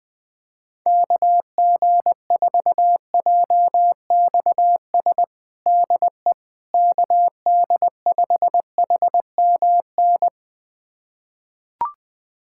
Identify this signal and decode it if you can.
Morse code, heard as a keyed tone, about 21 wpm, 700 Hz. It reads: KG4JXS DE KD5HMN
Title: USA Callsign DE Callsign - 6 Characters - 20wpm